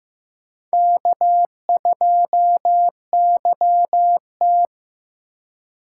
Morse code, heard as a keyed tone, about 18 wpm, 700 Hz. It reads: K2YT
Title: Speed Racing - USA 1x2 Call Signs 15wpm